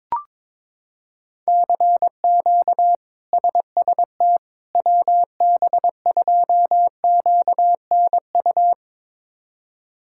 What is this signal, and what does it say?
Morse code, heard as a keyed tone, about 22 wpm, 700 Hz. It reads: CQ SST WB2QNU
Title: K1USN SST QSO Practice - 22wpm